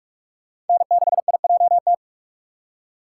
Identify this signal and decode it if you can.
Morse code, heard as a keyed tone, about 45 wpm, 700 Hz. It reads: N6IJT
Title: USA Call Signs - 5 Characters - 6x Character Spacing - 45wpm